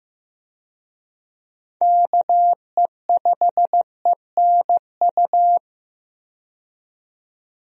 Morse code, heard as a keyed tone, about 15 wpm, 700 Hz. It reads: KE5ENU
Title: USA Call Signs - 6 Characters - 15wpm